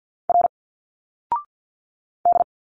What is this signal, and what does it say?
Morse code, heard as a keyed tone, about 51 wpm, 700 Hz. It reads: R D
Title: Single Letters 50wpm